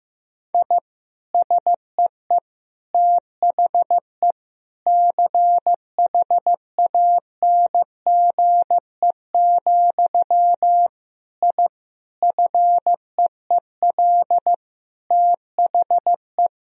Morse code, heard as a keyed tone, about 16 wpm, 700 Hz. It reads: I SEE THE CHANGE, I FEEL THE
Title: Sentences from Top 1000 Words - Review Part 1 - 15wpm